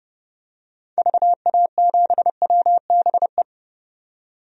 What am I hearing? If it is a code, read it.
Morse code, heard as a keyed tone, about 32 wpm, 700 Hz. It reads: VA7WBE
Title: DXCC-100 Call Signs - 6 Characters - 30wpm